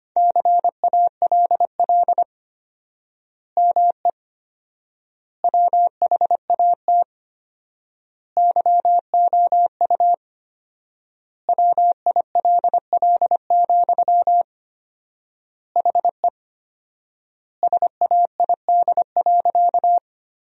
Morse code, heard as a keyed tone, about 25 wpm, 700 Hz. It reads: CALL ME WHAT YOU WILL, HE SAID.
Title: 4x Word-Spacing - Sentences from Top 200 Words - 25wpm